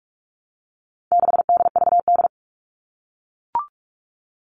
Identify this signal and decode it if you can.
Morse code, heard as a keyed tone, about 44 wpm, 700 Hz. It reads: 6DVD
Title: All Cut Numbers - 4-Digit 45wpm